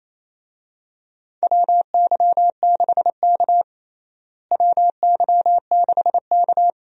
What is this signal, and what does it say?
Morse code, heard as a keyed tone, about 28 wpm, 700 Hz. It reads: WY6K WY6K
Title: USA Call Signs - 4 Characters - Repeated 3x - 28wpm